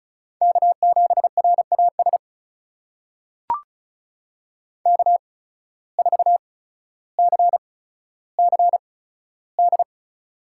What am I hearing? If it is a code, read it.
Morse code, heard as a keyed tone, about 36 wpm, spaced as for 15 wpm, 700 Hz. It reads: K7RAS K4CCD
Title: USA Call Signs - 5 Characters - 8x Character Spacing - 35wpm